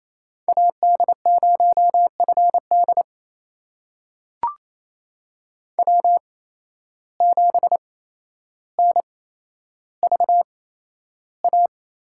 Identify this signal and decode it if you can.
Morse code, heard as a keyed tone, about 28 wpm, spaced as for 12 wpm, 700 Hz. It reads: AD0FD W7NVA
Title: USA Call Signs - 8x Character Spacing - 28wpm